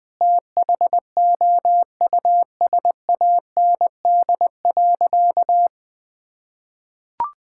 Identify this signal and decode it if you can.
Morse code, heard as a keyed tone, about 21 wpm, 700 Hz. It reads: THOUSAND.
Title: Sentences from Top 1000 Words - Review Part 5 - 20wpm